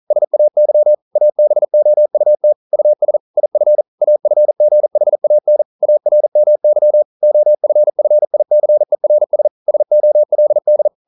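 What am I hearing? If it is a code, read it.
Morse code, heard as a keyed tone, about 42 wpm, 600 Hz. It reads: SAY ABOUT US IF AFGHAN ARMY OFFICERS SOLD
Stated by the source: Morse code practice at 41 wpm, from CW med Gnister.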